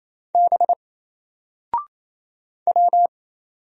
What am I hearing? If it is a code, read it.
Morse code, heard as a keyed tone, about 27 wpm, 700 Hz. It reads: B W
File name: Single-Letter-Number-28wpm